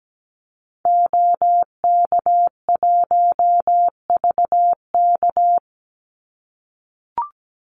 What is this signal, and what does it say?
Morse code, heard as a keyed tone, about 17 wpm, 700 Hz. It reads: OK1VK
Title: Call Signs - 17wpm